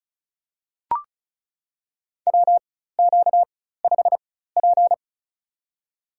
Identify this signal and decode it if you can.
Morse code, heard as a keyed tone, about 36 wpm, spaced as for 16 wpm, 700 Hz. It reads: WQ5P
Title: USA Call Signs - 4 Characters - 4x Character Spacing - 35wpm